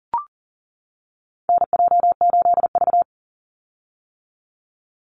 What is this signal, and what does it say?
Morse code, heard as a keyed tone, about 39 wpm, 700 Hz. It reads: NJ8V
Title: Call Signs - 40wpm